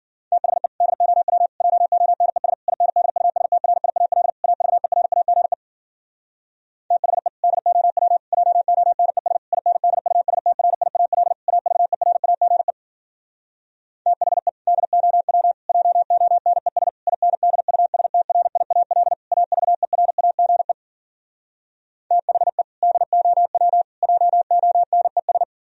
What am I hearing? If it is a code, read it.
Morse code, heard as a keyed tone, about 52 wpm, 700 Hz. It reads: THE DOW JONES INDUSTRIAL AVERAGE THE DOW JONES INDUSTRIAL AVERAGE THE DOW JONES INDUSTRIAL AVERAGE THE DOW JONES